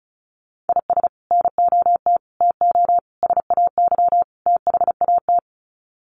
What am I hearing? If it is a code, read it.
Morse code, heard as a keyed tone, about 34 wpm, 700 Hz. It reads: IS NOT TO SAY THAT
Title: Sets of 5 Words 35wpm